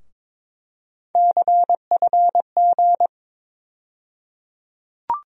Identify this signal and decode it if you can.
Morse code, heard as a keyed tone, about 22 wpm, 700 Hz. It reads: CFG
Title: USA Call Sign Postfixes - 3 Letters 22wpm